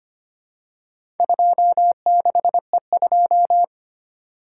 Morse code, heard as a keyed tone, about 25 wpm, 700 Hz. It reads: 26E2
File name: practice / 25wpm / Four Letter-Number Combinations - 25wpm